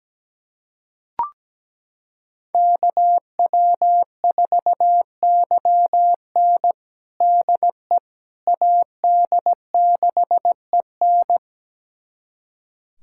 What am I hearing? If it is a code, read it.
Morse code, heard as a keyed tone, about 17 wpm, 700 Hz. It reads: KW4YN DE AD6EN
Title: USA Callsign DE Callsign - 5 Characters - 17wpm